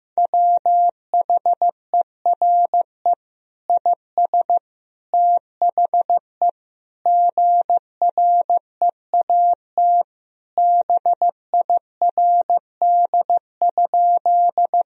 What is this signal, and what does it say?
Morse code, heard as a keyed tone, about 16 wpm, 700 Hz. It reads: WHERE IS THE GREAT BIRD?
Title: Sentences from Top 1000 Words - Review Part 2 - 15wpm